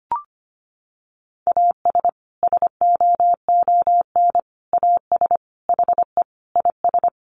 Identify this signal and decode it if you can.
Morse code, heard as a keyed tone, about 25 wpm, 700 Hz. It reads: AS SOON AS HE IS
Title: Sentences from Top 1500 Words 25wpm